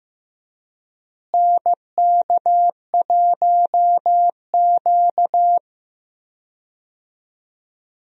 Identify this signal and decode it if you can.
Morse code, heard as a keyed tone, about 15 wpm, 700 Hz. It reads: NK1Q
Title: USA Call Signs - 4 Characters - Repeated 3x - 15wpm